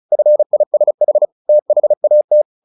Morse code, heard as a keyed tone, about 34 wpm, 600 Hz. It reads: FISH THAT
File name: the old man and the sea / the old man and the sea 1 35wpm